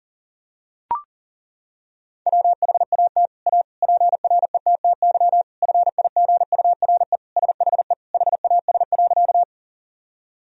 Morse code, heard as a keyed tone, about 37 wpm, 700 Hz. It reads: WHAT A PRETTY FIGURE SHE HAS.
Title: Sentences from Top 1000 Words - Review Part 2 - 40wpm